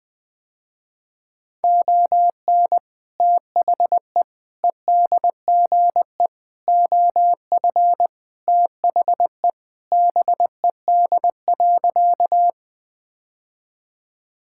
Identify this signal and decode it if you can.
Morse code, heard as a keyed tone, about 20 wpm, 700 Hz. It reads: ON THE EDGE OF THE BED.
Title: Sentences with 3-Word Phrases - Spoken Beginning - 20wpm